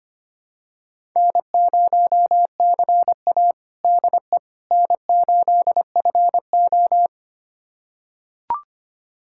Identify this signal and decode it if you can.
Morse code, heard as a keyed tone, about 25 wpm, 700 Hz. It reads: N0CA DE N8FO
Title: USA Callsign DE Callsign - 4 Characters - 25wpm